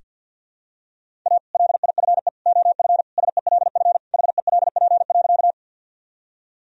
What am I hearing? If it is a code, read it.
Morse code, heard as a keyed tone, about 47 wpm, 700 Hz. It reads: A LIFE OF SELF HELP.
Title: Sentences from Top 1000 Words - Review Part 1 - 50wpm